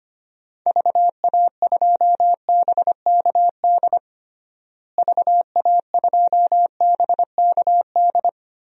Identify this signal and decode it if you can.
Morse code, heard as a keyed tone, about 25 wpm, 700 Hz. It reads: VA2BKD VA2BKD
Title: DXCC-100 Call Signs - 6 Characters - Repeated 2x - 25wpm